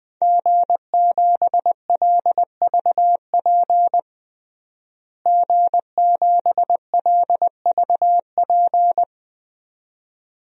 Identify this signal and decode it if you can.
Morse code, heard as a keyed tone, about 20 wpm, 700 Hz. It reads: G7LVP G7LVP
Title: DXCC-100 Call Signs - 5 Characters - Repeated 2x - 20wpm